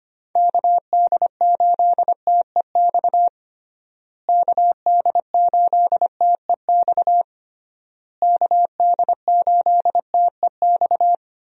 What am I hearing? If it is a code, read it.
Morse code, heard as a keyed tone, about 25 wpm, 700 Hz. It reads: KD8TEX KD8TEX KD8TEX
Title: USA Call Signs - 6 Characters - Repeated 3x - 25wpm